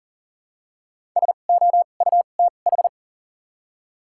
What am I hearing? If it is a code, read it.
Morse code, heard as a keyed tone, about 37 wpm, spaced as for 28 wpm, 700 Hz. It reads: SOUTH
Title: 2x Character Spacing - Top 500 Words - 40wpm